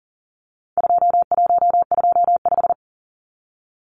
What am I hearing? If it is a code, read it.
Morse code, heard as a keyed tone, about 40 wpm, 700 Hz. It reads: 2125
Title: Numbers - 4-Digit 40wpm